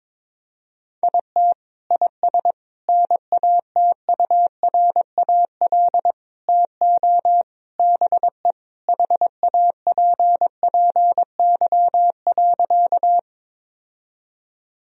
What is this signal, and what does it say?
Morse code, heard as a keyed tone, about 21 wpm, 700 Hz. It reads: IT IS NATURAL TO BE HAPPY.
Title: Sentences from Top 1000 Words - Review Part 4 - 22wpm